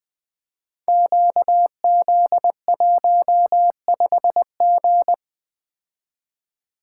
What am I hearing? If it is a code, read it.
Morse code, heard as a keyed tone, about 20 wpm, 700 Hz. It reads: QZ15G
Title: Code Groups - 5 Letter:Numbers - 20wpm